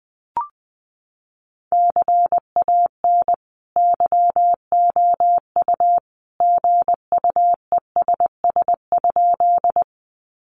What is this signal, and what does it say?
Morse code, heard as a keyed tone, about 20 wpm, 700 Hz. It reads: CAN YOU GUESS?
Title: Sentences from Top 1500 Words 20wpm